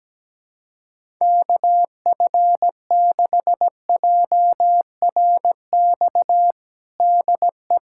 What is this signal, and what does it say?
Morse code, heard as a keyed tone, about 17 wpm, 700 Hz. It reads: KF6JRX DE
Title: USA Callsign DE Callsign 17wpm